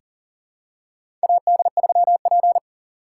Morse code, heard as a keyed tone, about 37 wpm, 700 Hz. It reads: AD3P
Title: USA Call Signs - 4 Characters - 2x Character Spacing - 40wpm